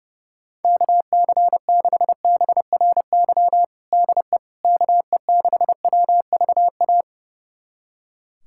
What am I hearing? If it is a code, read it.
Morse code, heard as a keyed tone, about 29 wpm, 700 Hz. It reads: KC6BRY DE KE6WVA
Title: USA Callsign DE Callsign - 6 Characters - 30wpm